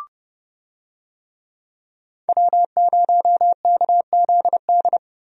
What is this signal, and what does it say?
Morse code, heard as a keyed tone, about 31 wpm, 700 Hz. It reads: W0KZD
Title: USA Call Signs - 5 Characters - Repeated 2x - 30wpm